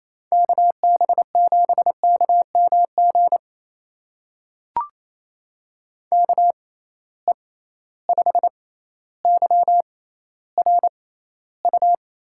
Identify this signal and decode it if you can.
Morse code, heard as a keyed tone, about 28 wpm, spaced as for 14 wpm, 700 Hz. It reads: KB7KMG KE5YRU
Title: USA Call Signs - 6 Characters - 6x Character Spacing - 28wpm